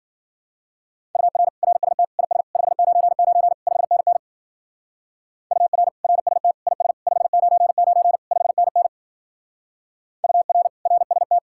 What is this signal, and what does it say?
Morse code, heard as a keyed tone, about 47 wpm, 700 Hz. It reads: UR RST IS 599 5NN UR RST IS 599 5NN UR RST
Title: Speed-Racing Common QSO Phrases - 40wpm